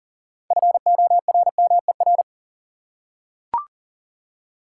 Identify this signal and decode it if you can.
Morse code, heard as a keyed tone, about 40 wpm, 700 Hz. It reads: FORMER
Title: Six Letter Words - 40wpm